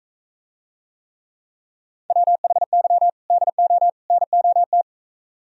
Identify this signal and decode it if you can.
Morse code, heard as a keyed tone, about 43 wpm, 700 Hz. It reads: WHY DO NOT